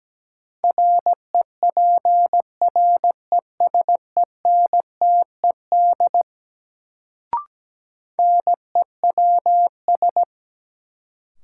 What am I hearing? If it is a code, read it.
Morse code, heard as a keyed tone, about 17 wpm, 700 Hz. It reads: REPRESENTED NEWS
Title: Sets of 5 Words - New Word Drill - 17wpm